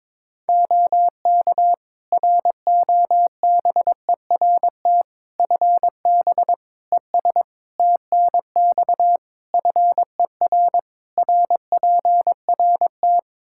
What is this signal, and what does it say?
Morse code, heard as a keyed tone, about 22 wpm, 700 Hz. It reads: OK ROBERT FB ES TNX FER RPRT
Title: K1USN SST QSO Practice 22wpm